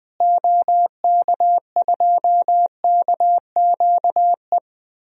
Morse code, heard as a keyed tone, about 21 wpm, 700 Hz. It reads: OK2KQE
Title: DXCC-100 Call Signs - 6 Characters - Repeated 2x - 20wpm